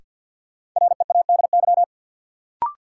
Morse code, heard as a keyed tone, about 53 wpm, 700 Hz. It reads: READY
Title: Sentences from Top 400 Words - New Word Drill - 50wpm